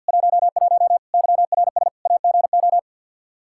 Morse code, heard as a keyed tone, about 49 wpm, 700 Hz. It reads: 11 YRS AGO
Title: Practice QSO - Exchange 3 - More Info - 50wpm